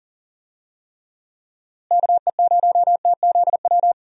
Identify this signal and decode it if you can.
Morse code, heard as a keyed tone, about 37 wpm, 700 Hz. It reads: KE0TZW
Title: USA Call Signs - 6 Characters - Repeated 3x - 40wpm